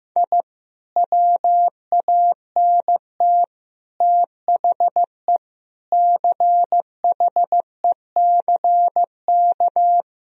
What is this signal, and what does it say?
Morse code, heard as a keyed tone, about 15 wpm, 700 Hz. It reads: I WANT THE CHECK
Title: Sentences from Top 1000 Words - Review Part 3 - 15wpm